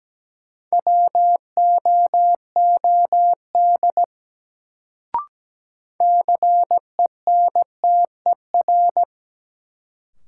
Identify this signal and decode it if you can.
Morse code, heard as a keyed tone, about 17 wpm, 700 Hz. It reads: WOOD CENTER
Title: Sentences from Top 1000 Words - Review Part 2 - New Word Drill - 17wpm